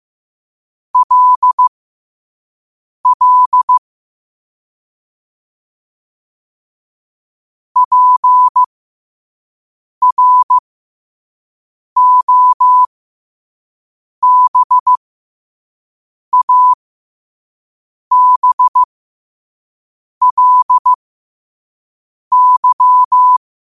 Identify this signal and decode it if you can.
Morse code, heard as a keyed tone, about 15 wpm, spaced as for 5 wpm, 1000 Hz. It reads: LL PROBABLY
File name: qotdfiles / 5wpm